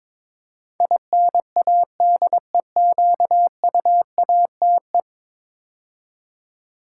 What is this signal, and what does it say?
Morse code, heard as a keyed tone, about 22 wpm, 700 Hz. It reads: INADEQUATE
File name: practice / 22wpm / 10 Letter Words - First 3 Letters Given - 22wpm